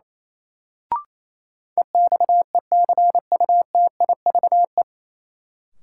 Morse code, heard as a keyed tone, about 28 wpm, 700 Hz. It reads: EXECUTIVE
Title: Sets of 5 Words Encore - New Word Drill - 28wpm